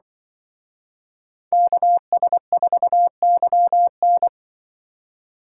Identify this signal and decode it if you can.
Morse code, heard as a keyed tone, about 23 wpm, 700 Hz. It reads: KS4YN
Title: Speed Racing - USA 2x2 Call Signs 20wpm